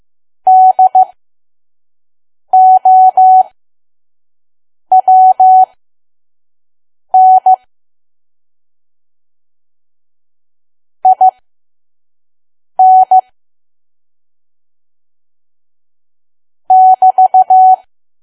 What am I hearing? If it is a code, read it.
Morse code, heard as a keyed tone, about 15 wpm, spaced as for 5 wpm, 750 Hz. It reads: DOWN IN <BT>
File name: Archive / 5 WPM files / PM